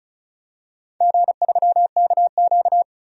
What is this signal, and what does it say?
Morse code, heard as a keyed tone, about 36 wpm, 700 Hz. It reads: G3KQ
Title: DXCC-100 Call Signs - 4 Characters - 35wpm